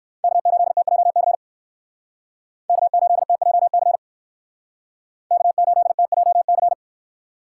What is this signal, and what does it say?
Morse code, heard as a keyed tone, about 57 wpm, 700 Hz. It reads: K8TJC K8TJC K8TJC